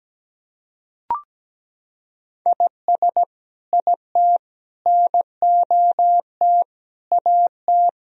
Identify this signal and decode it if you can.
Morse code, heard as a keyed tone, about 17 wpm, 700 Hz. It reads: IS IT NOT AT
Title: Sentences from Top 1000 Words - Review Part 4 - 17wpm